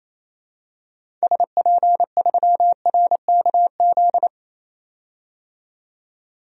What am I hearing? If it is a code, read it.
Morse code, heard as a keyed tone, about 28 wpm, 700 Hz. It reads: SP3RKZ
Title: DXCC-100 Call Signs - 6 Characters - 28wpm